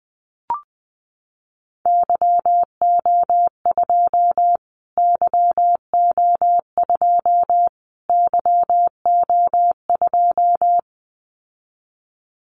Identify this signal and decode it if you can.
Morse code, heard as a keyed tone, about 20 wpm, 700 Hz. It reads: YO2 YO2 YO2
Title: Callsign Prefixes for Top 25 DXCC - Repeated 3x - 20wpm